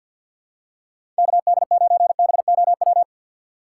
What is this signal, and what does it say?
Morse code, heard as a keyed tone, about 48 wpm, 700 Hz. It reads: KD9BOW
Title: USA Call Signs - 2x Character Spacing - 50wpm